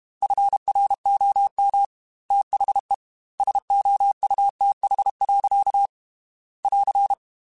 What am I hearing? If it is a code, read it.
Morse code, heard as a keyed tone, about 32 wpm, 800 Hz. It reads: FROM THE SOUTH. <AR>
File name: Weather Sets / Weather Sets-mp3s / Weather 4 10 @30